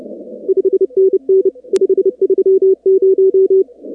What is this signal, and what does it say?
Morse code, heard as a keyed tone, about 29 wpm, 380 Hz. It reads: 5NN 530
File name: contests / 2019-cw-wpx-kt5j-k5pi- -2330 0-45